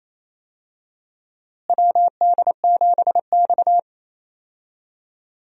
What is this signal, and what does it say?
Morse code, heard as a keyed tone, about 28 wpm, 700 Hz. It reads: WD7X
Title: USA Call Signs - 4 Characters - Repeated 2x - 28wpm